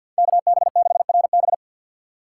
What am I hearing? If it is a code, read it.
Morse code, heard as a keyed tone, about 50 wpm, 700 Hz. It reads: KB6RB
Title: USA Call Signs - 5 Characters - 2x Character Spacing - 50wpm